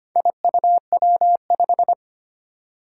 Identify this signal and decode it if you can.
Morse code, heard as a keyed tone, about 25 wpm, 700 Hz. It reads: IUW5
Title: Four Challenging Letter-Number Combinations 25wpm